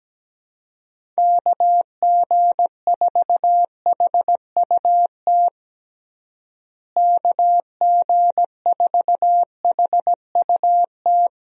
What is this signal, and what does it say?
Morse code, heard as a keyed tone, about 17 wpm, 700 Hz. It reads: KG4HUT KG4HUT
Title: USA Call Signs - 6 Characters - Repeated 3x - 17wpm